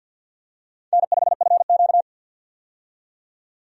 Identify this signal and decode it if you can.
Morse code, heard as a keyed tone, about 51 wpm, 700 Hz. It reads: N5FQ
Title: USA Call Signs - 4 Characters - 50wpm